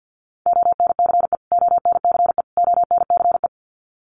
Morse code, heard as a keyed tone, about 50 wpm, 700 Hz. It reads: ONCE ONCE ONCE
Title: Top 500 Words Repeated 50wpm